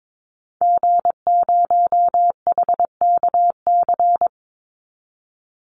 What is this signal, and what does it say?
Morse code, heard as a keyed tone, about 22 wpm, 700 Hz. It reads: G0HKC
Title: Call Signs Repeated 2x - 22wpm